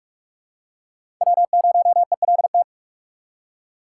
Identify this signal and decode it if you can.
Morse code, heard as a keyed tone, about 46 wpm, 700 Hz. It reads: W0ELT
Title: USA Call Signs - 3x Character Spacing - 45wpm